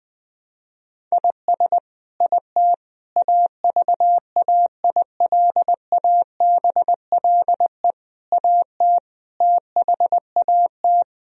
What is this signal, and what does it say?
Morse code, heard as a keyed tone, about 20 wpm, 700 Hz. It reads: IS IT AVAILABLE AT THAT
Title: Sentences from Top 1000 Words - Review Part 3 - 20wpm